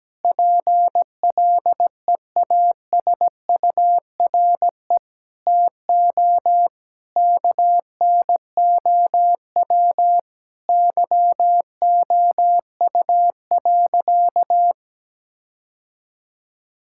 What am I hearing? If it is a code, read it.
Morse code, heard as a keyed tone, about 17 wpm, 700 Hz. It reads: PLEASURE TO KNOW YOU.
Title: Sentences from Top 1600 Words 17wpm